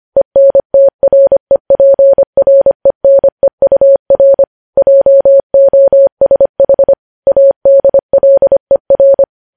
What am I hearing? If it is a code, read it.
Morse code, heard as a keyed tone, about 25 wpm, 550 Hz. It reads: ENTREPRENEUR JOSH ADLER
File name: podcast / News headlines-1x-25x25